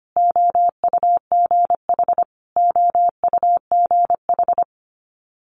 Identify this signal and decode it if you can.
Morse code, heard as a keyed tone, about 25 wpm, 700 Hz. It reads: OUGH OUGH
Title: Multi-letter Phonemes Repeated 3x 25wpm